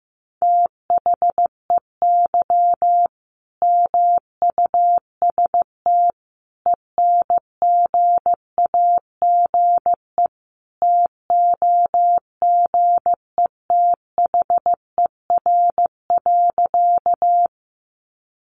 Morse code, heard as a keyed tone, about 15 wpm, 700 Hz. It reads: THEY MUST ENGAGE TOGETHER.
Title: Sentences from Top 1500 Words 15wpm